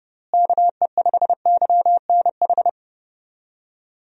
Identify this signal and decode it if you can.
Morse code, heard as a keyed tone, about 32 wpm, 700 Hz. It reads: KE5YNH
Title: USA Call Signs - 6 Characters - Repeated 3x - 30wpm